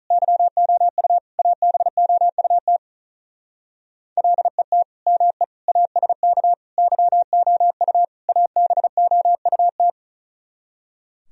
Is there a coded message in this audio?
Yes: Morse code from 0.1 to 9.9 s, heard as a keyed tone, about 45 wpm, 700 Hz. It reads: YOU ABOUT LET ME ASK YOU ABOUT